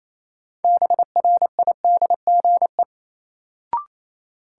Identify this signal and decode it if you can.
Morse code, heard as a keyed tone, about 28 wpm, 700 Hz. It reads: BRIDGE
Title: Sentences from Top 1200 Words - New Word Drill Alt - 28wpm